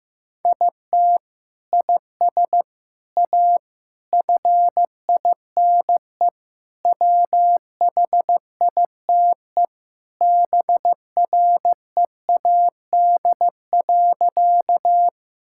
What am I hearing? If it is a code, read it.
Morse code, heard as a keyed tone, about 15 wpm, 700 Hz. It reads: IT IS A FINE WHITE BREAD.
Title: Sentences from Top 1600 Words 15wpm